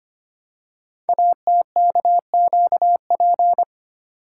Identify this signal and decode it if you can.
Morse code, heard as a keyed tone, about 25 wpm, 700 Hz. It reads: ATKQP
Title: Code Groups - 5 Letters - 25wpm